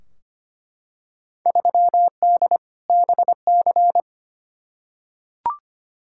Morse code, heard as a keyed tone, about 25 wpm, 700 Hz. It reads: 3D BC
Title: Field Day Exchanges Encore - 25wpm